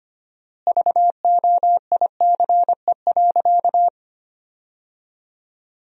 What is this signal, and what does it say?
Morse code, heard as a keyed tone, about 25 wpm, 700 Hz. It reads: VOICE.
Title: 4x Word-Spacing - Sentences from Top 500 Words - 25wpm